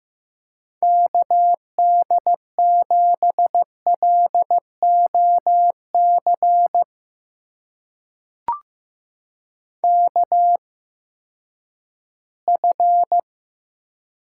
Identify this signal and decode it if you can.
Morse code, heard as a keyed tone, about 15 wpm, spaced as for 10 wpm, 700 Hz. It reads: KD7LOC KF
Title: USA Call Signs - 8x Character Spacing - 15wpm